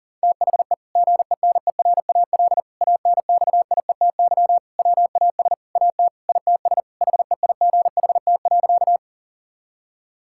Morse code, heard as a keyed tone, about 37 wpm, 700 Hz. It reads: THE GENERAL ANXIETY WAS AT ITS HEIGHT.
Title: Sentences from Top 1600 Words 40wpm